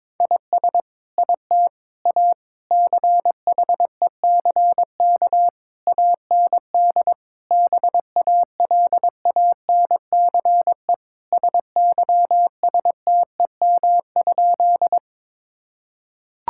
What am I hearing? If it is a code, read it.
Morse code, heard as a keyed tone, about 22 wpm, 700 Hz. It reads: IS IT A CHECK AND BALANCE SYSTEM?
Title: Sentences from Top 1000 Words - Review Part 6 - 22wpm